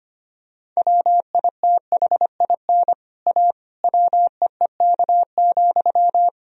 Morse code, heard as a keyed tone, about 25 wpm, 700 Hz. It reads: WITHIN A WEEK,
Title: Sentences from Top 1000 Words - Review Part 4 - 25wpm